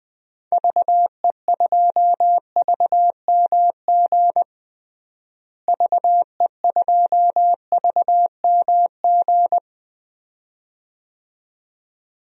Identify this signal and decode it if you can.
Morse code, heard as a keyed tone, about 20 wpm, 700 Hz. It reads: VE2VMG VE2VMG
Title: DXCC-100 Call Signs - 6 Characters - Repeated 2x - 20wpm